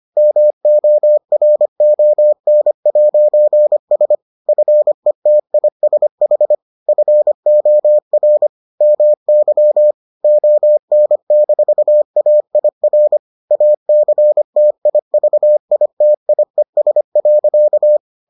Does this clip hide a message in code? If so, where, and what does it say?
Yes: Morse code from 0.2 to 18.0 s, heard as a keyed tone, about 25 wpm, 600 Hz. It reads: MORON'S FETISH FOR MY ON-AIR ACTIVITIES.